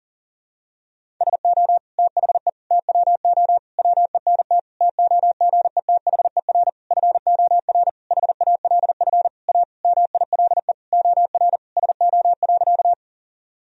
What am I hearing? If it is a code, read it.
Morse code, heard as a keyed tone, about 37 wpm, 700 Hz. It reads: SO THE TWO WENT TOGETHER FOR HALF A MILE OR SO.
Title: Sentences from Top 1000 Words - Review Part 1 - 40wpm